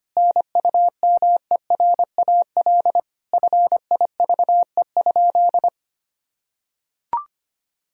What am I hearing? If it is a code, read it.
Morse code, heard as a keyed tone, about 25 wpm, 700 Hz. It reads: NUMERAL FIVE?
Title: Sentences from Top 1000 Words - Review Part 2 - 25wpm